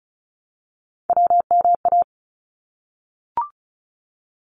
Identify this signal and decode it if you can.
Morse code, heard as a keyed tone, about 36 wpm, 700 Hz. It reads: WMA
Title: ARRL Sections 35wpm